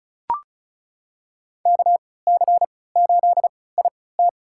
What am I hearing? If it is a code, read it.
Morse code, heard as a keyed tone, about 36 wpm, spaced as for 20 wpm, 700 Hz. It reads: KC8IT
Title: USA Call Signs - 3x Character Spacing - 35wpm